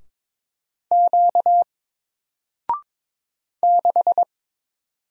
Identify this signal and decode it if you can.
Morse code, heard as a keyed tone, about 22 wpm, 700 Hz. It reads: Q 6
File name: Single-Letter-Number-22wpm